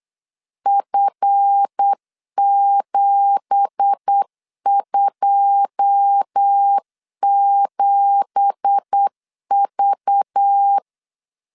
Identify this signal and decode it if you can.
Morse code, heard as a keyed tone, about 8 wpm, 800 Hz. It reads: F727V
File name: 8wpm